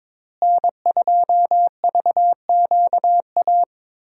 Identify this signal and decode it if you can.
Morse code, heard as a keyed tone, about 22 wpm, 700 Hz. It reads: N2VQA
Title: USA Call Signs - 5 Characters - Repeated 2x - 22wpm